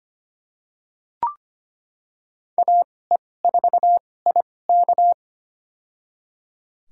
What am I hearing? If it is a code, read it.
Morse code, heard as a keyed tone, about 25 wpm, spaced as for 18 wpm, 700 Hz. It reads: AE4IK
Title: USA Call Signs - 5 Characters - 2x Character Spacing - 25wpm